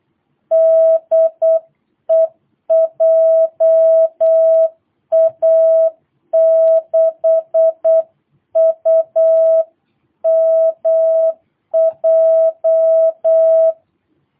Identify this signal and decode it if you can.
Morse code, heard as a keyed tone, about 8 wpm, 660 Hz. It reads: DE JA6UMJ
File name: jh8cbh / 2022cw